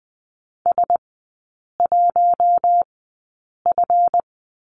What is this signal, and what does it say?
Morse code, heard as a keyed tone, about 20 wpm, spaced as for 8 wpm, 700 Hz. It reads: S1F
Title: 138 Lesson 23e - Letter f - Three Characters 20wpm